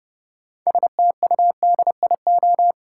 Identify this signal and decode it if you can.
Morse code, heard as a keyed tone, about 31 wpm, 700 Hz. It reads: STUDIO
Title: Sentences from Top 1200 Words - New Word Drill - 30wpm